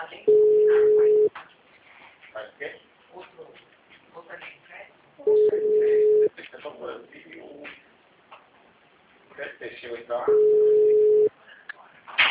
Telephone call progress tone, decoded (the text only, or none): ringback tone